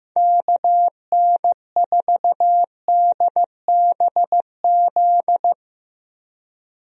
Morse code, heard as a keyed tone, about 16 wpm, 700 Hz. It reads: KN4DBZ
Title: USA Call Signs - 6 Characters - Repeated 2x - 15wpm